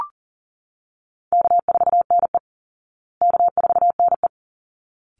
Morse code, heard as a keyed tone, about 40 wpm, 700 Hz. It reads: K4NE K4NE
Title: Call Signs Repeated 3x - 40wpm